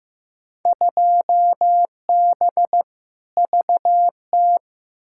Morse code, heard as a keyed tone, about 15 wpm, 700 Hz. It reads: 2B VT
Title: Field Day Exchanges Encore - 15wpm